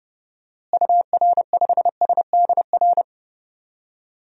Morse code, heard as a keyed tone, about 30 wpm, 700 Hz. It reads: UR5SDR
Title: DXCC-100 Call Signs - 6 Characters - Repeated 3x - 30wpm